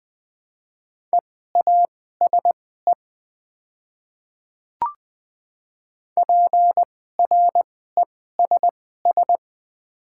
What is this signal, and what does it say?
Morse code, heard as a keyed tone, about 20 wpm, spaced as for 14 wpm, 690 Hz. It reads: EASE PRESS
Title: 2x Character Spacing - Top 500 Words - 20wpm